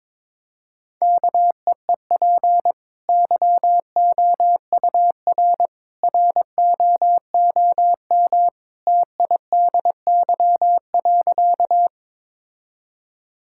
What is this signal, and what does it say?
Morse code, heard as a keyed tone, about 22 wpm, 700 Hz. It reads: KEEP YOUR ROOM TIDY.
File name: practice / 22wpm / Easy Sentences with Hints - 22wpm